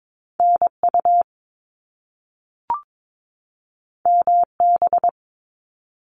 Morse code, heard as a keyed tone, about 22 wpm, 700 Hz. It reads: NU MB
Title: Canadian Provinces and Territories 22wpm